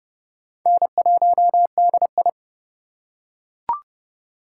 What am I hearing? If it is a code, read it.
Morse code, heard as a keyed tone, about 30 wpm, 700 Hz. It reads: N1DI
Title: USA Call Signs - 4x Character Spacing - 30wpm